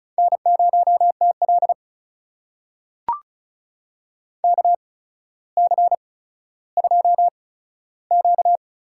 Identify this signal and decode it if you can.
Morse code, heard as a keyed tone, about 36 wpm, spaced as for 15 wpm, 700 Hz. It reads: N0TL KC2Q
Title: USA Call Signs - 8x Character Spacing - 35wpm